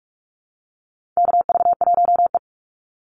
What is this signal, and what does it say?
Morse code, heard as a keyed tone, about 44 wpm, 700 Hz. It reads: KV1E
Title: Call Signs Repeated 2x - 45wpm